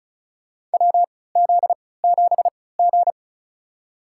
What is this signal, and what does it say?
Morse code, heard as a keyed tone, about 36 wpm, spaced as for 20 wpm, 700 Hz. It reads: WZ7G
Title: USA Call Signs - 4 Characters - 3x Character Spacing - 35wpm